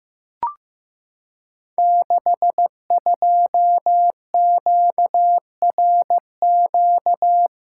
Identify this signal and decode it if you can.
Morse code, heard as a keyed tone, about 15 wpm, 700 Hz. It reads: 62QRQ
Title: Code Groups - 5 Letter:Numbers - 15wpm